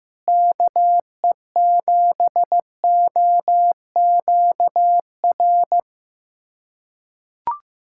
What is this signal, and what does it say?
Morse code, heard as a keyed tone, about 15 wpm, 700 Hz. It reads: KE7OQR
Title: USA Callsign DE Callsign - 6 Characters - 15wpm